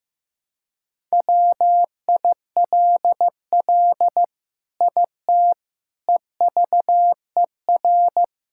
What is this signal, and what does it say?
Morse code, heard as a keyed tone, about 15 wpm, 700 Hz. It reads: WILL IT EVER
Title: Sentences from Top 1000 Words - Review Part 5 - 15wpm